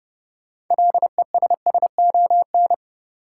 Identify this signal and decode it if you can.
Morse code, heard as a keyed tone, about 30 wpm, 700 Hz. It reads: LESSON
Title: Sentences from Top 1200 Words - New Word Drill - 30wpm